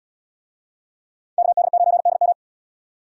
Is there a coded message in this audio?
Yes: Morse code from 1.4 to 2.3 s, heard as a keyed tone, about 72 wpm, 700 Hz. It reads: KD9DM